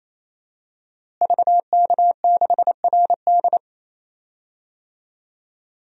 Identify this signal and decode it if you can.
Morse code, heard as a keyed tone, about 28 wpm, 700 Hz. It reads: VK6RD
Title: DXCC-100 Call Signs - 5 Characters - Repeated 3x - 28wpm